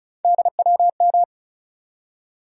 Morse code, heard as a keyed tone, about 34 wpm, 700 Hz. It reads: DWM
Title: USA Call Sign Postfixes - 3 Letters 35wpm